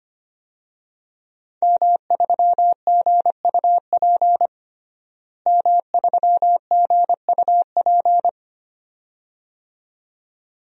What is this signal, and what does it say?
Morse code, heard as a keyed tone, about 25 wpm, 700 Hz. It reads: M3GUP M3GUP
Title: DXCC-100 Call Signs - 5 Characters - Repeated 2x - 25wpm